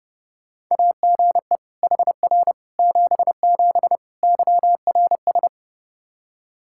Morse code, heard as a keyed tone, about 30 wpm, 700 Hz. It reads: AGE HR 77 YRS
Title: Practice QSO - Exchange 3 - More Info - 30wpm